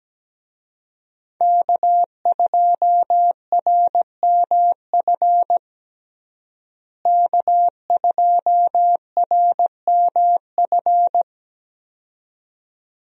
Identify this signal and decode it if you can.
Morse code, heard as a keyed tone, about 17 wpm, 700 Hz. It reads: K2RMF K2RMF
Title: USA Call Signs - 5 Characters - Repeated 2x - 17wpm